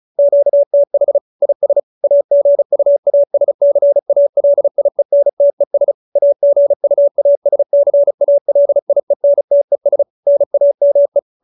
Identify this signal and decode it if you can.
Morse code, heard as a keyed tone, about 34 wpm, 570 Hz. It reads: QTH IS AGUASCALIENTES AGUASCALIENTES NAME